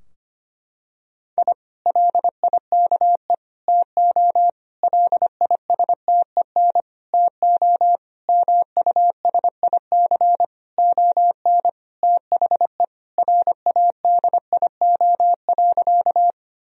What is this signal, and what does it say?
Morse code, heard as a keyed tone, about 25 wpm, 700 Hz. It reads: I LIKE TO LISTEN TO MUSIC ON THE RADIO.
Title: Sentences from Top 1000 Words - Review Part 3 - 25wpm